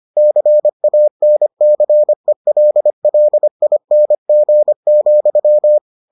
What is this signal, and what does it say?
Morse code, heard as a keyed tone, about 25 wpm, 600 Hz. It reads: CANCELLING,